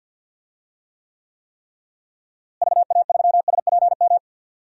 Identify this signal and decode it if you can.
Morse code, heard as a keyed tone, about 51 wpm, 700 Hz. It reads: VA3SPM
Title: DXCC-100 Call Signs - 6 Characters - Repeated 3x - 50wpm